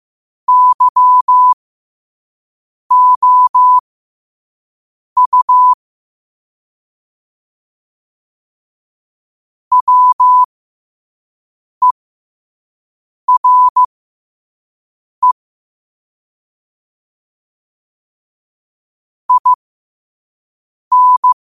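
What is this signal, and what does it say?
Morse code, heard as a keyed tone, about 15 wpm, spaced as for 5 wpm, 1000 Hz. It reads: YOU WERE IN
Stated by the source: Morse practice quote sent at 5 wpm.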